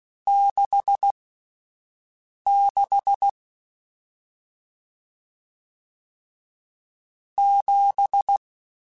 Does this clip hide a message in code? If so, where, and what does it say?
Yes: Morse code from 0.3 to 8.4 s, heard as a keyed tone, about 16 wpm, spaced as for 5 wpm, 800 Hz. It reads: 66 7